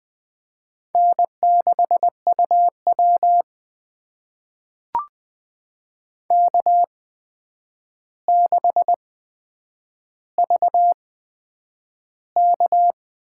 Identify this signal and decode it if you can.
Morse code, heard as a keyed tone, about 20 wpm, spaced as for 9 wpm, 700 Hz. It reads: N6UW K6VK
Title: USA Call Signs - 4 Characters - 8x Character Spacing - 20wpm